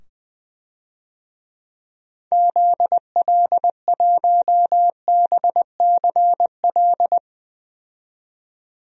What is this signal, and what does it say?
Morse code, heard as a keyed tone, about 20 wpm, 700 Hz. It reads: ZL1BCL
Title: DXCC-100 Call Signs - 6 Characters - 20wpm